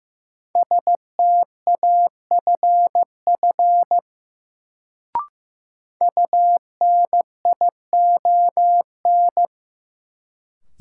Morse code, heard as a keyed tone, about 15 wpm, 700 Hz. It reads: STAFF UNION
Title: Sets of 5 Words Encore - New Word Drill - 15wpm